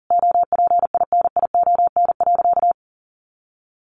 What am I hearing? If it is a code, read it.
Morse code, heard as a keyed tone, about 40 wpm, 700 Hz. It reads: OPINION.
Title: Sentences from Top 1200 Words 40wpm